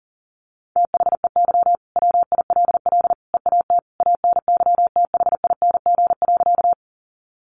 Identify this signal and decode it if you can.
Morse code, heard as a keyed tone, about 40 wpm, 700 Hz. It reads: THEY WILL EAT ANYTHING.
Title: Sentences from Top 1000 Words 40wpm